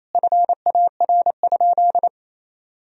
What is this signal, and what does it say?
Morse code, heard as a keyed tone, about 28 wpm, 700 Hz. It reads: FAR?
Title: Sentences from Top 1000 Words - Review Part 2 - 28wpm